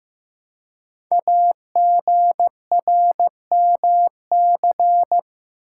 Morse code, heard as a keyed tone, about 15 wpm, 700 Hz. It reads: AGRMC
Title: Code Groups - 5 Letters - 15wpm